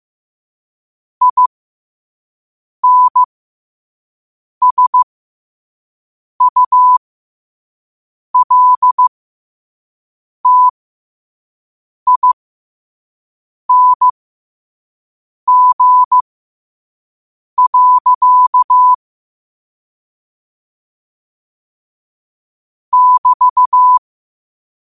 Morse code, heard as a keyed tone, about 15 wpm, spaced as for 5 wpm, 1000 Hz. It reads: INSULTING. <BT>